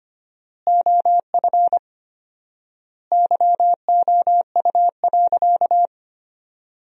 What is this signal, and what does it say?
Morse code, heard as a keyed tone, about 25 wpm, 700 Hz. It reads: OF YOU.
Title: 4x Word-Spacing - Sentences from Top 400 Words - 25wpm